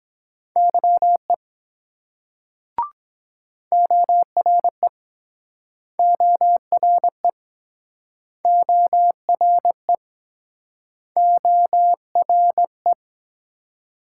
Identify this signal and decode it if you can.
Morse code, heard as a keyed tone, about 23 wpm, 700 Hz. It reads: YE ORE ORE ORE ORE